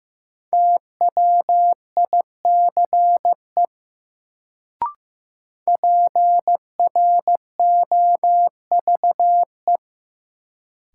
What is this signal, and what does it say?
Morse code, heard as a keyed tone, about 15 wpm, 700 Hz. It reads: TWICE PROVE
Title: Sentences from Top 1600 Words - New Word Drill Alt - 15wpm